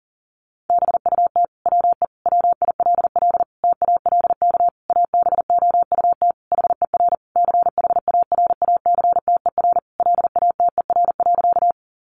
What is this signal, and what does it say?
Morse code, heard as a keyed tone, about 37 wpm, 700 Hz. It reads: BUT WE WILL TALK ABOUT HER CHARACTER LATER.
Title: Sentences from Top 1200 Words 40wpm